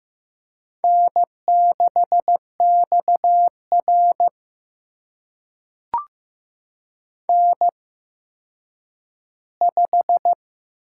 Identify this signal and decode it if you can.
Morse code, heard as a keyed tone, about 15 wpm, spaced as for 9 wpm, 700 Hz. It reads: N6XR N5
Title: USA Call Signs - 4 Characters - 8x Character Spacing - 15wpm